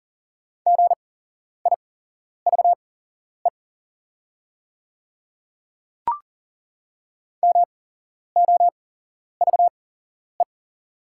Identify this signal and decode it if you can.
Morse code, heard as a keyed tone, about 40 wpm, spaced as for 11 wpm, 700 Hz. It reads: GIVE MOVE
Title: 8x Character Spacing - Top 200 Words - 40wpm